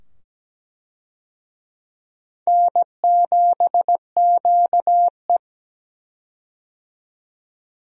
Morse code, heard as a keyed tone, about 17 wpm, 700 Hz. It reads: N7QE
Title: USA Call Signs - 4 Characters - 17wpm